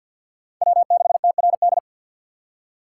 Morse code, heard as a keyed tone, about 47 wpm, 700 Hz. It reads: W6TRD
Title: USA Call Signs - 6x Character Spacing - 50wpm